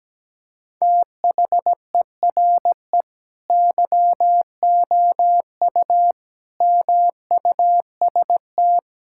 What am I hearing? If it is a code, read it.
Morse code, heard as a keyed tone, about 17 wpm, 700 Hz. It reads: THERE YOU MUST
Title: Sentences from Top 1600 Words 17wpm